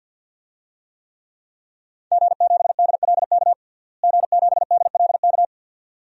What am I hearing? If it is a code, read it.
Morse code, heard as a keyed tone, about 51 wpm, 700 Hz. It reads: G7DLK G7DLK
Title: DXCC-100 Call Signs - 5 Characters - Repeated 2x - 50wpm